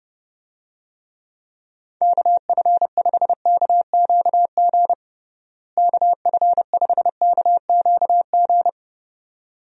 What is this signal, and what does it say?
Morse code, heard as a keyed tone, about 32 wpm, 700 Hz. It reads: KF5KQG KF5KQG
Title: USA Call Signs - 6 Characters - Repeated 2x - 30wpm